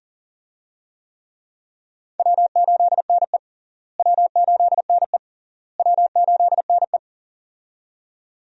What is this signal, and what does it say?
Morse code, heard as a keyed tone, about 40 wpm, 700 Hz. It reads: W8NE W8NE W8NE
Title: USA Call Signs - 4 Characters - Repeated 3x - 40wpm